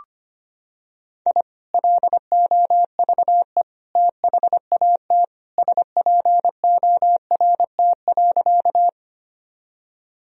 Morse code, heard as a keyed tone, about 25 wpm, 700 Hz. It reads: I LOVE THAT SPORT.
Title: Sentences from Top 1000 Words - Review Part 4 - 25wpm